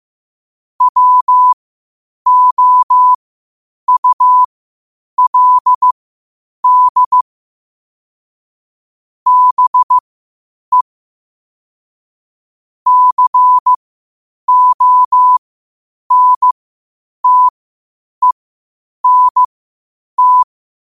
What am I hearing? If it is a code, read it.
Morse code, heard as a keyed tone, about 15 wpm, spaced as for 8 wpm, 1000 Hz. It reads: WOULD BE CONTENT
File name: qotdfiles / 8wpm